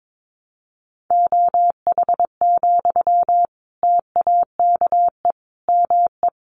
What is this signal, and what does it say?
Morse code, heard as a keyed tone, about 22 wpm, 700 Hz. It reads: OH, TAKE ME
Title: Sentences from Top 500 Words 22wpm